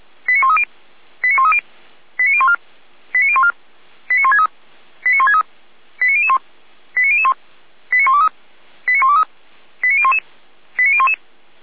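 Five-tone selective call (ZVEI), heard as quick runs of five tones, tones of about 70 ms. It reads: 89130 89130 89014 89014 89163 89163 89001 89001 89123 89123 89011 89011
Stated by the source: ZVEI five-tone selective call 89010, received on an SDR, from the Feuerwehr Orlamünde fire brigade audio server.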